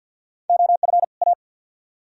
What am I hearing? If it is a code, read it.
Morse code, heard as a keyed tone, about 69 wpm, spaced as for 59 wpm, 700 Hz. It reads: OF A
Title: Speed-Racing - Sets of 2 Words 50wpm